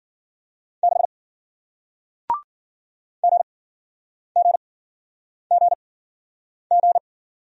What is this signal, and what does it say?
Morse code, heard as a keyed tone, about 53 wpm, spaced as for 9 wpm, 700 Hz. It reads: 6 GGGG